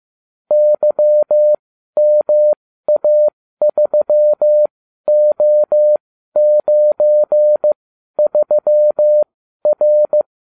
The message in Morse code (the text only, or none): YMA3O93R